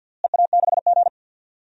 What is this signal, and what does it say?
Morse code, heard as a keyed tone, about 47 wpm, 700 Hz. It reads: EA6G
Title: DXCC-100 Call Signs - 4 Characters - Repeated 2x - 50wpm